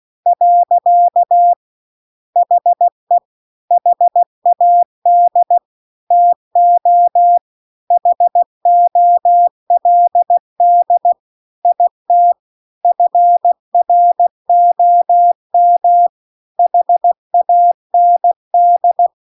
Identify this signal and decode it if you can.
Morse code, heard as a keyed tone, about 16 wpm, 700 Hz. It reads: . HE HAD TO HOLD IT FROM HAND
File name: CW-The-Big-Twoheaded-River-2-16WPM